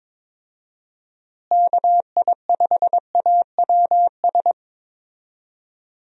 Morse code, heard as a keyed tone, about 22 wpm, 700 Hz. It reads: KI5AWS
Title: USA Call Signs - 6 Characters - Repeated 2x - 22wpm